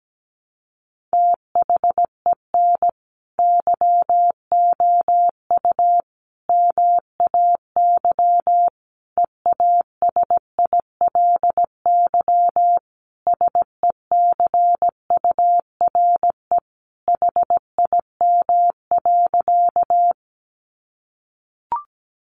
Morse code, heard as a keyed tone, about 17 wpm, 700 Hz. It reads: THEN YOU MAY EASILY SECURE HIM.
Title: Sentences from Top 1200 Words 17wpm